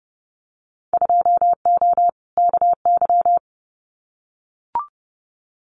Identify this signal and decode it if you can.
Morse code, heard as a keyed tone, about 30 wpm, 700 Hz. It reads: 2O KY
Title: Winter Field Day Exchanges 30wpm